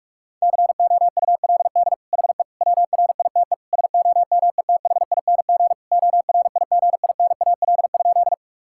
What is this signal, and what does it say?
Morse code, heard as a keyed tone, about 46 wpm, 700 Hz. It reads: COULD HE WRITE SOMETHING ORIGINAL?
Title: Sentences from Top 1000 Words - Review Part 6 - 45wpm